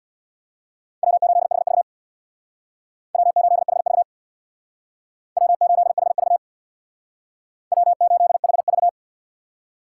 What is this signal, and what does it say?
Morse code, heard as a keyed tone, about 64 wpm, 700 Hz. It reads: W8HV W8HV W8HV W8HV